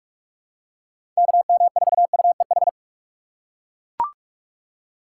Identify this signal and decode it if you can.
Morse code, heard as a keyed tone, about 41 wpm, 700 Hz. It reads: KM4UEH
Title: USA Call Signs - 6 Characters - 4x Character Spacing - 45wpm